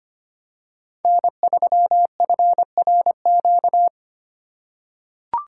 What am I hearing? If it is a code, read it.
Morse code, heard as a keyed tone, about 25 wpm, 700 Hz. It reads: N3FRQ
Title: USA Call Signs - 3x Character Spacing - 25wpm